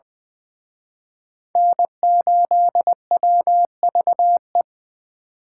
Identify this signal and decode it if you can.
Morse code, heard as a keyed tone, about 23 wpm, 700 Hz. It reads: N8WVE
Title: Speed Racing - USA 1x3 Call Signs 20wpm